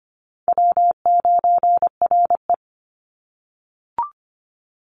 Morse code, heard as a keyed tone, about 25 wpm, 700 Hz. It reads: W9RE
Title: Easy Call Signs - 25wpm